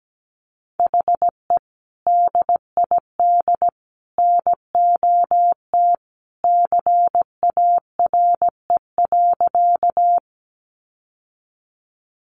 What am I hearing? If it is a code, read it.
Morse code, heard as a keyed tone, about 17 wpm, 700 Hz. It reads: HE DID NOT CARE.
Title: Sentences from Top 600 Words 17wpm